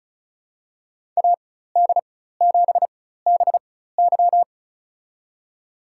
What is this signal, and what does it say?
Morse code, heard as a keyed tone, about 36 wpm, spaced as for 16 wpm, 700 Hz. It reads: AD7BY
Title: USA Call Signs - 5 Characters - 4x Character Spacing - 35wpm